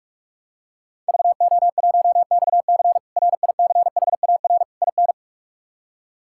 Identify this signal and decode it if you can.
Morse code, heard as a keyed tone, about 43 wpm, 700 Hz. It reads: VO1XC RICHAR IN
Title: K1USN SST QSO Practice - 45wpm